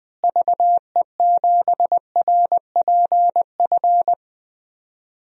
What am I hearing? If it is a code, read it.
Morse code, heard as a keyed tone, about 20 wpm, 700 Hz. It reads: VE7RPF
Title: DXCC-100 Call Signs - 6 Characters - Repeated 2x - 20wpm